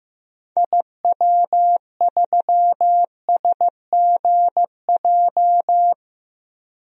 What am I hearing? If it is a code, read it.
Morse code, heard as a keyed tone, about 15 wpm, 700 Hz. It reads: IW3SGJ
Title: DXCC-100 Call Signs - 6 Characters - 15wpm